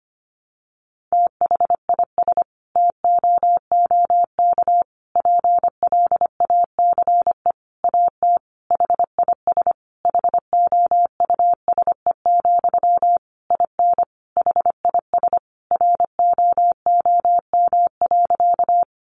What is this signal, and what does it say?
Morse code, heard as a keyed tone, about 25 wpm, 700 Hz. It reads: THIS TOOK PLACE AT HIS HOUSE, IN HIS ROOM.
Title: Sentences from Top 300 Words 25wpm